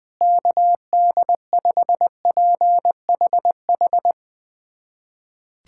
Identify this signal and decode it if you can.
Morse code, heard as a keyed tone, about 21 wpm, 700 Hz. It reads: KD5PHH
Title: USA Callsign DE Callsign 20wpm